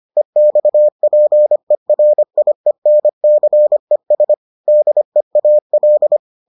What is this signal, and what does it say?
Morse code, heard as a keyed tone, about 25 wpm, 600 Hz. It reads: EXPERIENCES DEAL